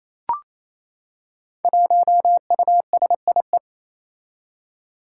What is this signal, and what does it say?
Morse code, heard as a keyed tone, about 28 wpm, 700 Hz. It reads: 1USIE
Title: Five Challenging Letter-Number Combinations 28wpm